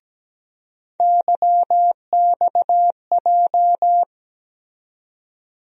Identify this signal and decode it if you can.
Morse code, heard as a keyed tone, about 17 wpm, 700 Hz. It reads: YXJ
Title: USA Call Sign Postfixes - 3 Letters 17wpm